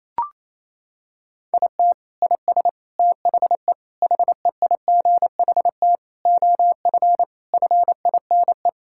Morse code, heard as a keyed tone, about 28 wpm, 700 Hz. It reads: IT IS THE HEIGHT OF FINE
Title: Sentences from Top 1600 Words 28wpm